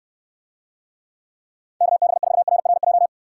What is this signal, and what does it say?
Morse code, heard as a keyed tone, about 69 wpm, 700 Hz. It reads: KB4RRJ